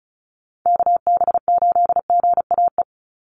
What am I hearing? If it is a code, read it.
Morse code, heard as a keyed tone, about 34 wpm, 700 Hz. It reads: KB8GAE
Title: Call Signs Repeated 2x - 35wpm